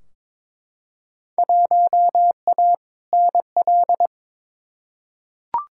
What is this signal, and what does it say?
Morse code, heard as a keyed tone, about 21 wpm, 700 Hz. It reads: 1A NL
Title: Field Day Exchanges Encore - 22wpm